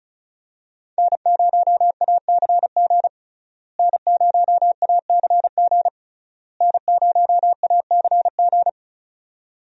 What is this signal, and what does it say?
Morse code, heard as a keyed tone, about 36 wpm, 700 Hz. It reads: N0ACG N0ACG N0ACG
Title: USA Call Signs - 5 Characters - Repeated 3x - 35wpm